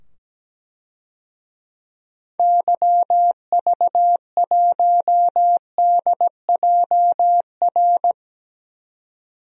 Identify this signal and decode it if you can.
Morse code, heard as a keyed tone, about 17 wpm, 700 Hz. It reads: YV1DJR
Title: DXCC-100 Call Signs - 6 Characters - 17wpm